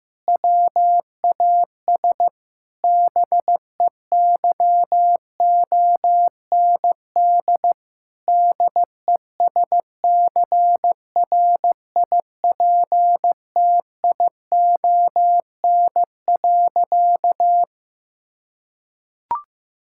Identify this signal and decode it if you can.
Morse code, heard as a keyed tone, about 15 wpm, 700 Hz. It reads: WAS BEYOND DESCRIPTION.
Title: Sentences from Top 1600 Words 15wpm